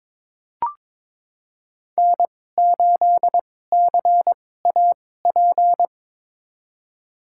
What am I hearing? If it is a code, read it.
Morse code, heard as a keyed tone, about 22 wpm, spaced as for 16 wpm, 700 Hz. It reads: N8CAP
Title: USA Call Signs - 2x Character Spacing - 22wpm